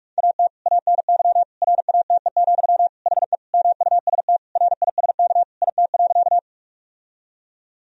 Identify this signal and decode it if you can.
Morse code, heard as a keyed tone, about 45 wpm, 700 Hz. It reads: AT ANY RATE, HE MUST RISK IT.
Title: Sentences from Top 1000 Words - Review Part 4 - 45wpm